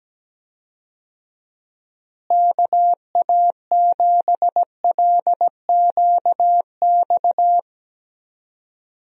Morse code, heard as a keyed tone, about 17 wpm, 700 Hz. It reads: KA7LQX
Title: USA Call Signs - 6 Characters - Repeated 2x - 17wpm